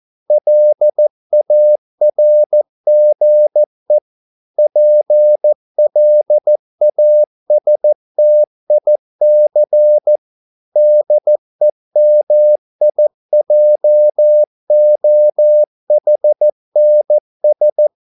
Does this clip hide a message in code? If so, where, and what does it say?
Yes: Morse code from 0.3 to 17.9 s, heard as a keyed tone, about 14 wpm, 600 Hz. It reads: LARGE PLASTIC DEMIJOHNS